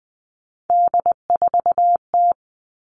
Morse code, heard as a keyed tone, about 20 wpm, 700 Hz. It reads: D4T
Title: All Cut Numbers - 3-Digit 20wpm